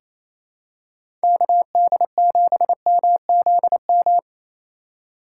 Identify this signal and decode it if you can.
Morse code, heard as a keyed tone, about 28 wpm, 700 Hz. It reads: KD7MZM
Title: USA Call Signs - 3x Character Spacing - 28wpm